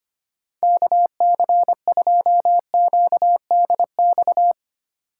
Morse code, heard as a keyed tone, about 25 wpm, 700 Hz. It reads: KC2QDX
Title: USA Call Signs - 6 Characters - 4x Character Spacing - 25wpm